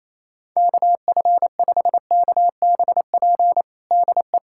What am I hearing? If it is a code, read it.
Morse code, heard as a keyed tone, about 28 wpm, 700 Hz. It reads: KF5KBP DE
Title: USA Callsign DE Callsign 28wpm